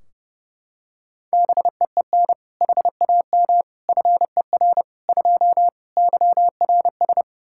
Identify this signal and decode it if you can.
Morse code, heard as a keyed tone, about 30 wpm, 700 Hz. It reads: BEEN HAM FER 2 YRS
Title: Practice QSO - Exchange 3 - More Info - 30wpm